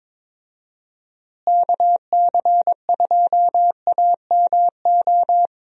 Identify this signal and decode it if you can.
Morse code, heard as a keyed tone, about 22 wpm, 700 Hz. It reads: KC2AMO
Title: USA Call Signs - 6 Characters - Repeated 2x - 22wpm